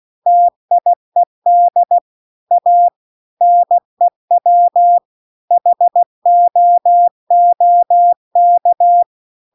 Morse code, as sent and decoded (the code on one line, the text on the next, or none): - .. -.. / .- / -. .-- / .... --- --- -.-
TIED A NEW HOOK